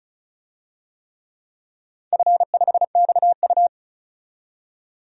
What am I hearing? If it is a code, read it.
Morse code, heard as a keyed tone, about 36 wpm, 700 Hz. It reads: F5XU
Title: DXCC-100 Call Signs - 4 Characters - 35wpm